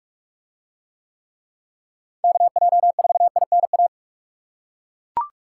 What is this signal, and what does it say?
Morse code, heard as a keyed tone, about 41 wpm, 700 Hz. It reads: KJ4INA